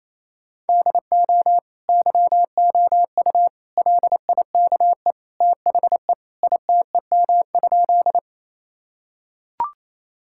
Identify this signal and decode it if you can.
Morse code, heard as a keyed tone, about 28 wpm, 700 Hz. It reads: DO YOU LIKE THE ITEM?
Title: Sentences from Top 1000 Words - Review Part 6 - 28wpm